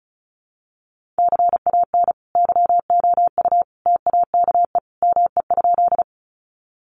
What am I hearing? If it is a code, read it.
Morse code, heard as a keyed tone, about 34 wpm, 700 Hz. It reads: CAN YOU TAKE ME?
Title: Sentences from Top 200 Words 35wpm